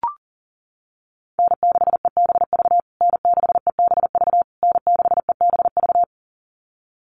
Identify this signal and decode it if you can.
Morse code, heard as a keyed tone, about 40 wpm, 700 Hz. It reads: N6EBV N6EBV N6EBV
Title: USA Call Signs Repeated 3x 40wpm